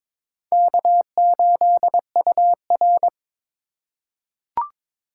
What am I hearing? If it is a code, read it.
Morse code, heard as a keyed tone, about 23 wpm, 700 Hz. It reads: K8UR
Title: USA Call Signs - 4 Characters - 3x Character Spacing - 22wpm